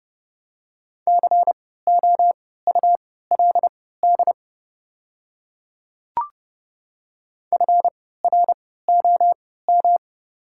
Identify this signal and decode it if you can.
Morse code, heard as a keyed tone, about 30 wpm, spaced as for 17 wpm, 700 Hz. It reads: COULD FROM
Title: 3x Character Spacing - Top 100 Words - 30wpm